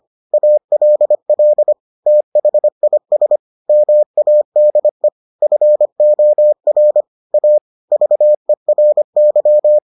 Morse code, heard as a keyed tone, about 25 wpm, 600 Hz. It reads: ALL THIS MADE FOR A VERY